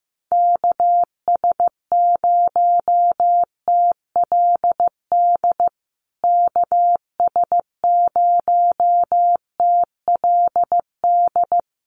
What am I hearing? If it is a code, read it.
Morse code, heard as a keyed tone, about 15 wpm, 700 Hz. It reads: KS0TLD KS0TLD
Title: USA Call Signs Repeated 3x 15wpm